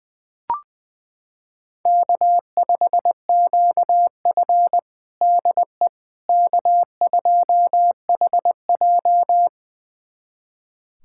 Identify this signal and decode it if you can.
Morse code, heard as a keyed tone, about 21 wpm, 700 Hz. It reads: K5QF DE K2HJ
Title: USA Callsign DE Callsign - 4 Characters - 20wpm